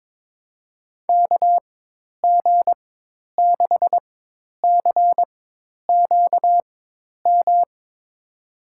Morse code, heard as a keyed tone, about 21 wpm, spaced as for 10 wpm, 700 Hz. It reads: KG6CQM
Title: USA Call Signs - 6 Characters - 4x Character Spacing - 22wpm